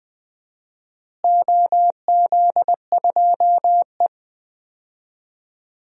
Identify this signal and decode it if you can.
Morse code, heard as a keyed tone, about 21 wpm, 700 Hz. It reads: OZ2E
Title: DXCC-100 Call Signs - 4 Characters - Repeated 3x - 20wpm